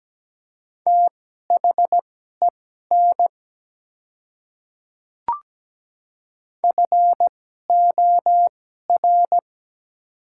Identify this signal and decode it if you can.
Morse code, heard as a keyed tone, about 17 wpm, spaced as for 12 wpm, 700 Hz. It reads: THEN FOR
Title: 2x Character Spacing - Top 100 Words - 17wpm